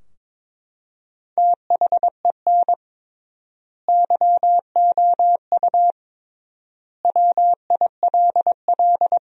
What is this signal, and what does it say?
Morse code, heard as a keyed tone, about 23 wpm, 700 Hz. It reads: THEN YOU WILL
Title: 3x Word-Spacing - Sentences from Top 400 Words - 22wpm